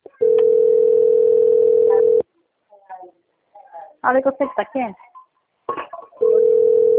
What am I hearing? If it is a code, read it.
Ringback tone (North America), 440 + 480 Hz, with speech.
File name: out- - - - - .10932